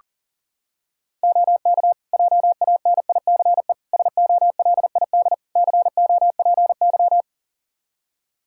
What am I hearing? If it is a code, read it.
Morse code, heard as a keyed tone, about 40 wpm, 700 Hz. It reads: OK JANICE SOLID COPY
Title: Practice QSO - Exchange 3 - More Info - 40wpm